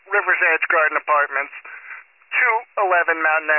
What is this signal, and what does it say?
Emergency Alert System (SAME) tones, heard as harsh buzzing bursts, with speech. It is heard from 0.2 to 3.4 s.